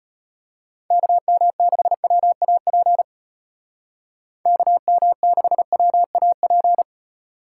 Morse code, heard as a keyed tone, about 30 wpm, 700 Hz. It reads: KM6WAP KM6WAP
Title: Speed Racing - USA Call Signs 25wpm